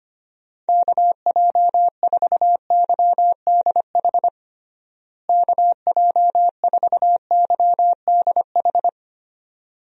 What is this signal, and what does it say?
Morse code, heard as a keyed tone, about 25 wpm, 700 Hz. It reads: KJ4YDH KJ4YDH
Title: USA Call Signs - 6 Characters - Repeated 2x - 25wpm